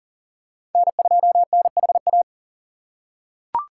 Morse code, heard as a keyed tone, about 40 wpm, 700 Hz. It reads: N2NHA
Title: USA Call Signs - 5 Characters - 6x Character Spacing - 40wpm